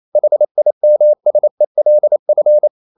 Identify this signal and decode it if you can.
Morse code, heard as a keyed tone, about 28 wpm, 600 Hz. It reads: HIMSELF
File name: CW / Practice / EnglishQuotations0228